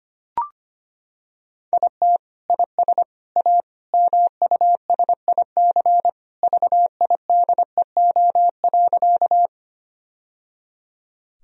Morse code, heard as a keyed tone, about 25 wpm, 700 Hz. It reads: IT IS A MUSIC VIDEO.
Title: Sentences from Top 1000 Words - Review Part 4 - 25wpm